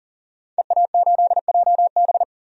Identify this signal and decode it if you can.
Morse code, heard as a keyed tone, about 40 wpm, 700 Hz. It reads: EA8JB
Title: DXCC-100 Call Signs - 5 Characters - 40wpm